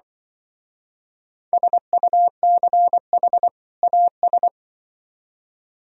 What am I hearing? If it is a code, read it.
Morse code, heard as a keyed tone, about 25 wpm, 700 Hz. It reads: SUCH AS
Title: Speed-Racing - Sets of 2 Words 20wpm